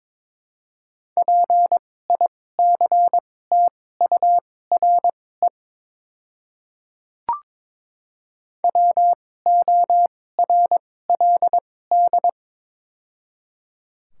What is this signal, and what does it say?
Morse code, heard as a keyed tone, about 22 wpm, spaced as for 16 wpm, 700 Hz. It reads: PICTURE WORLD
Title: 2x Character Spacing - Top 200 Words - 22wpm